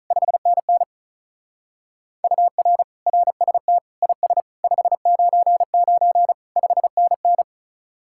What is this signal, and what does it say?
Morse code, heard as a keyed tone, about 44 wpm, 700 Hz. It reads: 5NN UR RST IS 599 5NN